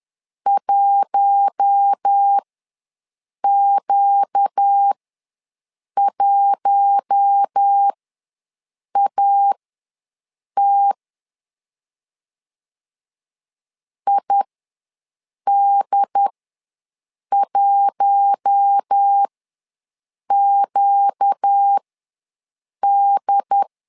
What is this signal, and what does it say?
Morse code, heard as a keyed tone, about 10 wpm, spaced as for 6 wpm, 800 Hz. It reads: 1Q1AT ID1QD
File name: LEZIONI / 4 Q1 ws 10wpm 2538